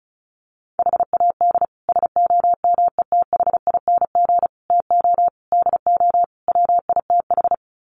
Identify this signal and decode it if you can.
Morse code, heard as a keyed tone, about 34 wpm, 700 Hz. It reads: HAD SOMETHING TO DO WITH
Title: Sets of 5 Words 35wpm